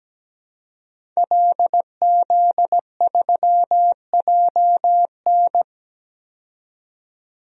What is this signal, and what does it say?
Morse code, heard as a keyed tone, about 17 wpm, 700 Hz. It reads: LZ3JN
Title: DXCC-100 Call Signs - 5 Characters - 17wpm